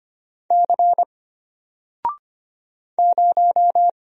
Morse code, heard as a keyed tone, about 25 wpm, 700 Hz. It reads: C 0
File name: Single-Letter-Number-25wpm